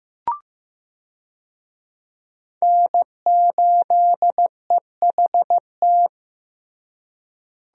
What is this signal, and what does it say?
Morse code, heard as a keyed tone, about 16 wpm, 700 Hz. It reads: N8EHT
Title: USA Call Signs - 5 Characters - Repeated 3x - 15wpm